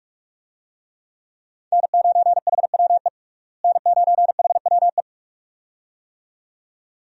Morse code, heard as a keyed tone, about 46 wpm, 700 Hz. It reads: N9HWE N9HWE
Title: USA Call Signs - 5 Characters - Repeated 2x - 45wpm